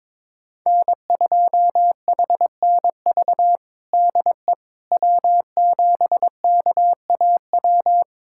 Morse code, heard as a keyed tone, about 21 wpm, 700 Hz. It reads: N2HNV DE W7KAW
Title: USA Callsign DE Callsign - 5 Characters - 22wpm